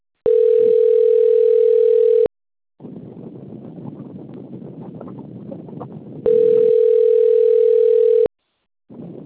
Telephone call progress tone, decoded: ringback tone

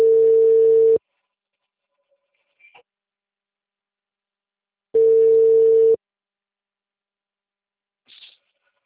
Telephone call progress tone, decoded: ringback tone

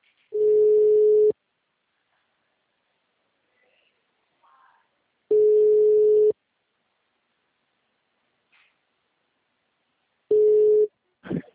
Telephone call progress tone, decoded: ringback tone

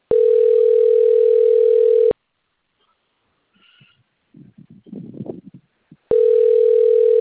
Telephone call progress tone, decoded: ringback tone

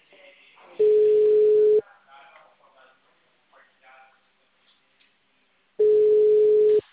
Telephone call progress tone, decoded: ringback tone

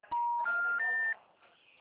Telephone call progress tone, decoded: special information tone

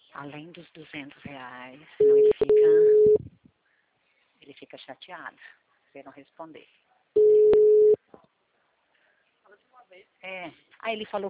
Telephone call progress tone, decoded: ringback tone

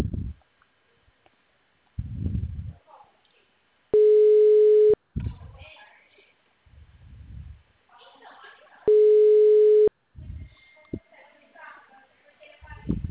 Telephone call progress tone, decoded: ringback tone

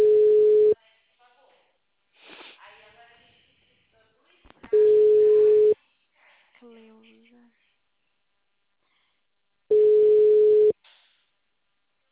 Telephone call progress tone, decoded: ringback tone

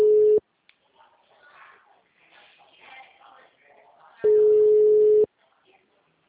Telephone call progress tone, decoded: ringback tone

